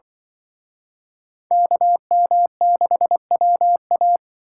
Morse code, heard as a keyed tone, about 25 wpm, 700 Hz. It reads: KM6WA